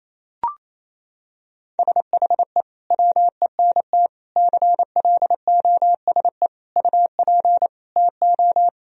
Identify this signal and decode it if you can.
Morse code, heard as a keyed tone, about 28 wpm, 700 Hz. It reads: SHE WENT CLOSE UP TO
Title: Sentences from Top 1000 Words - Review Part 1 - 28wpm